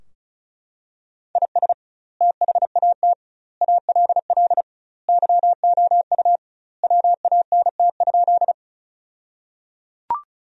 Morse code, heard as a keyed tone, about 36 wpm, 700 Hz. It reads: IS THAT ALL YOU WANT?
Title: 2x Word-Spacing - Sentences from Top 200 Words - 35wpm